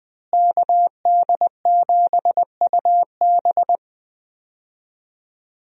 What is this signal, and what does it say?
Morse code, heard as a keyed tone, about 21 wpm, 700 Hz. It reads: KD7UB
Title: USA Call Signs - 5 Characters - Repeated 2x - 20wpm